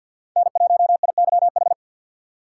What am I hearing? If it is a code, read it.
Morse code, heard as a keyed tone, about 52 wpm, 700 Hz. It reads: N1IYH
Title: USA Call Signs - 5 Characters - 2x Character Spacing - 50wpm